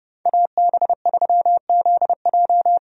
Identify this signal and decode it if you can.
Morse code, heard as a keyed tone, about 30 wpm, 700 Hz. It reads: AB3ZJ
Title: USA Call Signs - 5 Characters - 30wpm